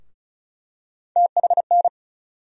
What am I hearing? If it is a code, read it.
Morse code, heard as a keyed tone, about 34 wpm, 700 Hz. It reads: THN
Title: USA Call Sign Postfixes - 3 Letters 35wpm